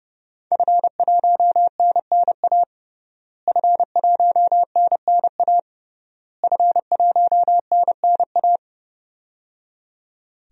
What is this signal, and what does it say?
Morse code, heard as a keyed tone, about 30 wpm, 700 Hz. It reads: F1NNA F1NNA F1NNA
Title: DXCC-100 Call Signs - 5 Characters - Repeated 3x - 30wpm